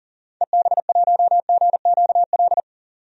Morse code, heard as a keyed tone, about 37 wpm, 700 Hz. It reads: EB1GQL
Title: DXCC-100 Call Signs - 6 Characters - 40wpm